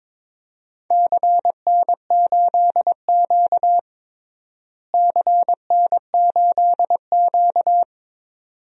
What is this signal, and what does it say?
Morse code, heard as a keyed tone, about 22 wpm, 700 Hz. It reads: CN8Q CN8Q
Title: DXCC-100 Call Signs - 4 Characters - Repeated 2x - 22wpm